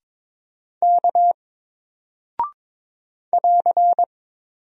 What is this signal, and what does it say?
Morse code, heard as a keyed tone, about 22 wpm, 700 Hz. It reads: K <AR>
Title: Single Character (Letter:Number:Punctuation:Prosign) Alt 22wpm